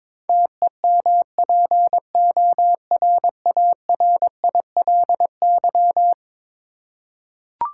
Morse code, heard as a keyed tone, about 22 wpm, 700 Hz. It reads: TEMPORARILY
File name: practice / 22wpm / Eleven Letter Words - 22wpm